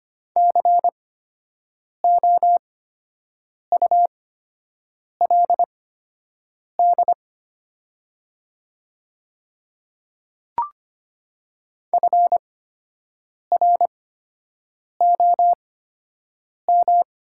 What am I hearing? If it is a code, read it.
Morse code, heard as a keyed tone, about 25 wpm, spaced as for 7 wpm, 700 Hz. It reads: COULD FROM
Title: 8x Character Spacing - Top 100 Words - 25wpm